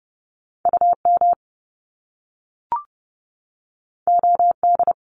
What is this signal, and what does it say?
Morse code, heard as a keyed tone, about 30 wpm, 700 Hz. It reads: UM OD
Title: Top Two Letter Combinations in Words 30wpm